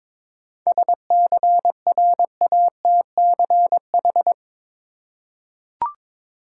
Morse code, heard as a keyed tone, about 22 wpm, 700 Hz. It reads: SCRATCH
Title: Seven Letter Words - 22wpm